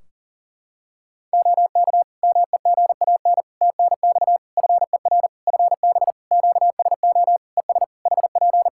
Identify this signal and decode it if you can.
Morse code, heard as a keyed tone, about 40 wpm, 700 Hz. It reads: OK MEGAN TNX FER FB QSO ES HP
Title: Practice QSO - Exchange 4 - Ending - 40wpm